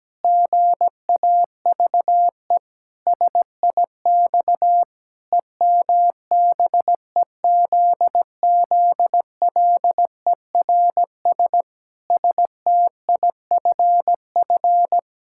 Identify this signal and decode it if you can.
Morse code, heard as a keyed tone, about 17 wpm, 700 Hz. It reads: GAVE SIX EMBEZZLERS STIFF
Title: Pangram Challenge v2 - 17wpm